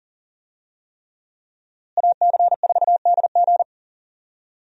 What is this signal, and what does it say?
Morse code, heard as a keyed tone, about 37 wpm, 700 Hz. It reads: AC4DG
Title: USA Call Signs - 5 Characters - Repeated 3x - 40wpm